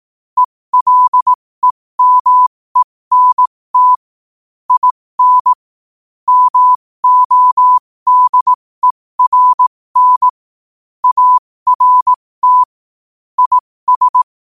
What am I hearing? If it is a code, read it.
Morse code, heard as a keyed tone, about 18 wpm, spaced as for 15 wpm, 1000 Hz. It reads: ELEMENT IN MODERN ART IS
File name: qotdfiles / 15wpm